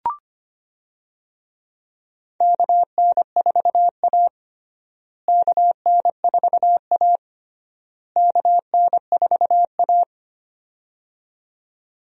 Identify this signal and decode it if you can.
Morse code, heard as a keyed tone, about 25 wpm, 700 Hz. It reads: KN4A KN4A KN4A
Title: USA Call Signs - 4 Characters - Repeated 3x - 25wpm